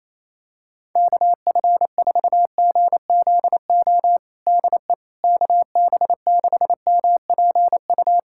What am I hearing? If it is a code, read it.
Morse code, heard as a keyed tone, about 27 wpm, 700 Hz. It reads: KF4GZO DE KB6MPU
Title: USA Callsign DE Callsign - 6 Characters - 28wpm